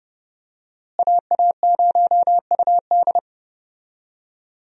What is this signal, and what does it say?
Morse code, heard as a keyed tone, about 30 wpm, 700 Hz. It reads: AA0UD
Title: USA Call Signs - 5 Characters - Repeated 3x - 30wpm